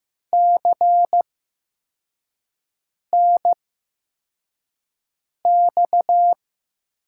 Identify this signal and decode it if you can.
Morse code, heard as a keyed tone, about 15 wpm, spaced as for 4 wpm, 700 Hz. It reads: CNX
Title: USA Call Signs - 8x Character Spacing - 15wpm